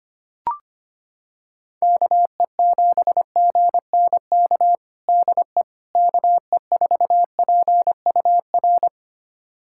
Morse code, heard as a keyed tone, about 25 wpm, 700 Hz. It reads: KE7GNK DE KE4PUR
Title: USA Callsign DE Callsign - 6 Characters - 25wpm